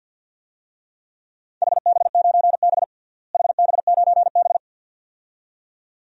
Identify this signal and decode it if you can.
Morse code, heard as a keyed tone, about 51 wpm, 700 Hz. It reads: HB9B HB9B
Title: DXCC-100 Call Signs - 4 Characters - Repeated 2x - 50wpm